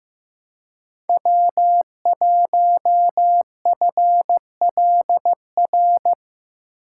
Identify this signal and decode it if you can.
Morse code, heard as a keyed tone, about 15 wpm, 700 Hz. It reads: W1FLR
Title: USA Call Signs - 2x Character Spacing - 15wpm